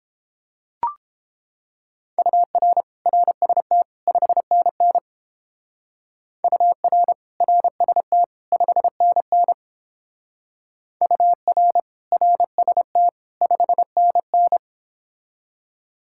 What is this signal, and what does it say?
Morse code, heard as a keyed tone, about 29 wpm, 700 Hz. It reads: UR RST 5NN UR RST 5NN UR RST 5NN